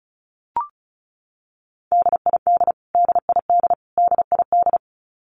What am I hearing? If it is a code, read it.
Morse code, heard as a keyed tone, about 34 wpm, 700 Hz. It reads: DID DID DID
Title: Top 200 Words Repeated 35wpm